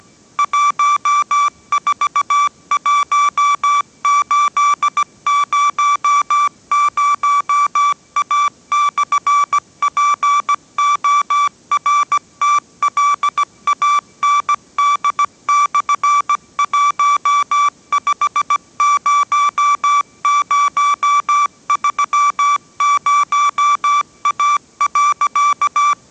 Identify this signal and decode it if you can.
Morse code, heard as a keyed tone, about 17 wpm, 1160 Hz. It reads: 141800A/PORTLAND/150030A.